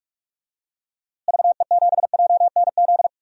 Morse code, heard as a keyed tone, about 45 wpm, 700 Hz. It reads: VE7JNZ
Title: DXCC-100 Call Signs - 6 Characters - Repeated 2x - 45wpm